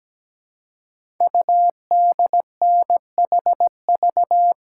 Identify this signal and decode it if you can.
Morse code, heard as a keyed tone, about 17 wpm, 700 Hz. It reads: UDNHV
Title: Code Groups - 5 Letters - 17wpm